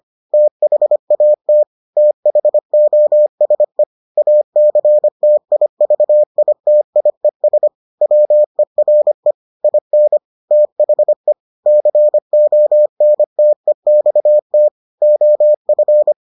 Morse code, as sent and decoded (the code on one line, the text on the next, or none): - .... .- - / - .... --- ... / .- -.-. - .. ...- .. - .. ... / .-- . .-. / .. -. / - .... / -.-. --- -. - . -..- - / --- ..-.
THAT THOSE ACTIVITIES WERE IN THE CONTEXT OF